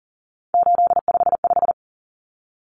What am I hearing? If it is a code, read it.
Morse code, heard as a keyed tone, about 38 wpm, 700 Hz. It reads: 855
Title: Numbers - 3-Digit 40wpm